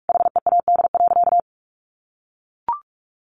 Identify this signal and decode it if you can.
Morse code, heard as a keyed tone, about 44 wpm, 700 Hz. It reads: HEAD.
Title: Sentences from Top 1100 Words 45wpm